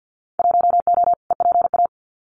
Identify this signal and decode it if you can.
Morse code, heard as a keyed tone, about 50 wpm, 700 Hz. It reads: 1O EPA
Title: Winter Field Day Exchanges 50wpm